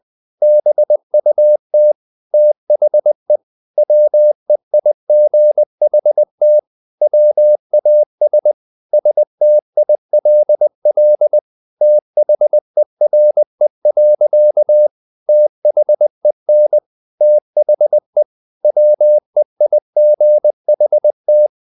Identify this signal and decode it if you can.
Morse code, heard as a keyed tone, about 20 wpm, 600 Hz. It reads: BUT THE WEIGHT WAS STILL THERE. THEN THE WEIGHT
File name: the old man and the sea / the old man and the sea part 1 20wpm